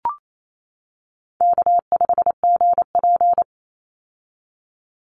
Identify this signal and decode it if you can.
Morse code, heard as a keyed tone, about 28 wpm, 700 Hz. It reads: K5GP
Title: Easy Call Signs - 28wpm